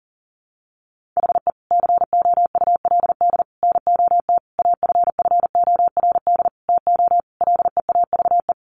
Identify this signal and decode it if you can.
Morse code, heard as a keyed tone, about 40 wpm, 700 Hz. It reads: HE COULD NOT AFFORD TO LEAVE
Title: Sentences from Top 1100 Words 40wpm